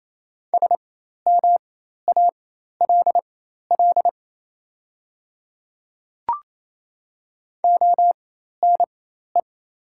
Morse code, heard as a keyed tone, about 28 wpm, spaced as for 13 wpm, 690 Hz. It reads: SMALL ONE
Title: 4x Character Spacing - Top 200 Words - 28wpm